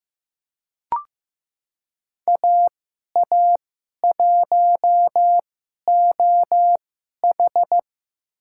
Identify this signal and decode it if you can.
Morse code, heard as a keyed tone, about 15 wpm, spaced as for 11 wpm, 700 Hz. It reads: AA1OH
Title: USA Call Signs - 2x Character Spacing - 15wpm